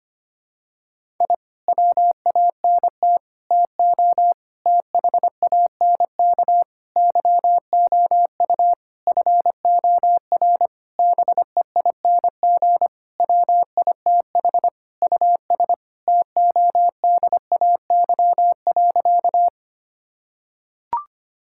Morse code, heard as a keyed tone, about 25 wpm, 700 Hz. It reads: I WANT TO THANK YOU FOR BEING WITH US TODAY.
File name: practice / 25wpm / Sentences with 9-Word Phrases - 25wpm